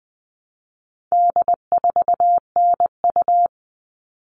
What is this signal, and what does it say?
Morse code, heard as a keyed tone, about 20 wpm, 700 Hz. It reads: D4NU
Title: All Cut Numbers - 4-Digit 20wpm